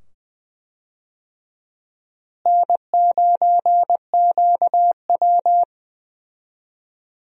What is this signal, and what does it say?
Morse code, heard as a keyed tone, about 20 wpm, 700 Hz. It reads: N9QW
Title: USA Call Signs - 4 Characters - Repeated 2x - 20wpm